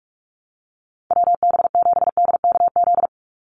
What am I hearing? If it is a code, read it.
Morse code, heard as a keyed tone, about 44 wpm, 700 Hz. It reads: WB7DKZ
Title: Call Signs Repeated 3x - 45wpm